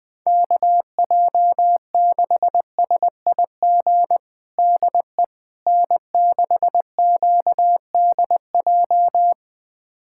Morse code, heard as a keyed tone, about 20 wpm, 700 Hz. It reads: KJ6SIG DE N6QDJ
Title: USA Callsign DE Callsign 20wpm